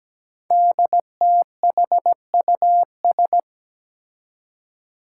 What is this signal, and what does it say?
Morse code, heard as a keyed tone, about 17 wpm, 700 Hz. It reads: DTHUS
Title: Five Challenging Letter-Number Combinations 17wpm